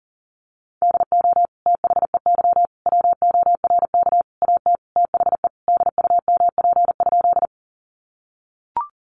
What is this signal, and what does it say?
Morse code, heard as a keyed tone, about 40 wpm, 700 Hz. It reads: DO THEY WORK AT THE DUMP?
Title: Sentences from Top 1300 Words 40wpm